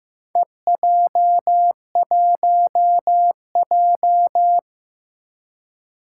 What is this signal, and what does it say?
Morse code, heard as a keyed tone, about 15 wpm, 700 Hz. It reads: EJ1J
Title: Four Challenging Letter-Number Combinations 15wpm